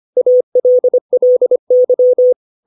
Morse code, heard as a keyed tone, about 25 wpm, 500 Hz. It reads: ALLY
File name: 632 words / 632 Words 25WPM